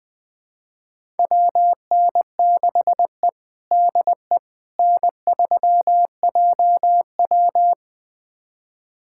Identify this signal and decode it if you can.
Morse code, heard as a keyed tone, about 20 wpm, 700 Hz. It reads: WN6E DE N3JW
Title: USA Callsign DE Callsign - 4 Characters - 20wpm